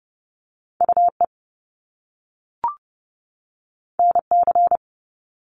Morse code, heard as a keyed tone, about 30 wpm, 700 Hz. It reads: UE NC
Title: Top Two Letter Combinations in Words 30wpm